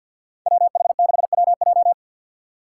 Morse code, heard as a keyed tone, about 53 wpm, 700 Hz. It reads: WH6WJ
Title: USA Call Signs - 5 Characters - 4x Character Spacing - 50wpm